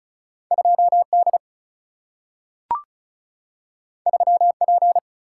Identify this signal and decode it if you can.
Morse code, heard as a keyed tone, about 36 wpm, 700 Hz. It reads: 2D 3P
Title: Single-Letter Single-Number 35wpm